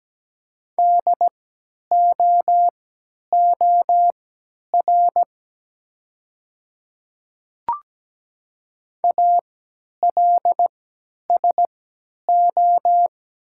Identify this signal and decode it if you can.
Morse code, heard as a keyed tone, about 17 wpm, spaced as for 10 wpm, 690 Hz. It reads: DOOR ALSO
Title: 3x Character Spacing - Top 300 Words - 17wpm